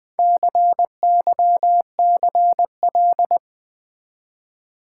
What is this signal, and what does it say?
Morse code, heard as a keyed tone, about 21 wpm, 700 Hz. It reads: CYCL
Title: Build A Word - 1 Letter At A Time - 3 to 5-Letter Words - 20wpm